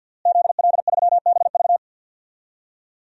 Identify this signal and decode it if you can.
Morse code, heard as a keyed tone, about 52 wpm, 700 Hz. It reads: ZL3BV
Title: DXCC-100 Call Signs - 5 Characters - Repeated 3x - 50wpm